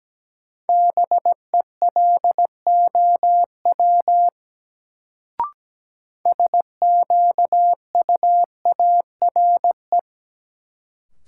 Morse code, heard as a keyed tone, about 17 wpm, 700 Hz. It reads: BELOW SQUARE
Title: Sentences from Top 1200 Words - New Word Drill Alt - 17wpm